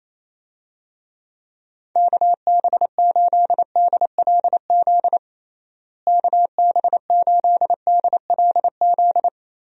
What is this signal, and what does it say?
Morse code, heard as a keyed tone, about 28 wpm, 700 Hz. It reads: KB8DLZ KB8DLZ
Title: USA Call Signs - 6 Characters - Repeated 3x - 28wpm